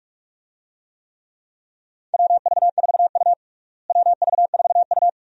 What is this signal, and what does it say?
Morse code, heard as a keyed tone, about 46 wpm, 700 Hz. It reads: WV4U WV4U
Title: USA Call Signs - 4 Characters - Repeated 2x - 45wpm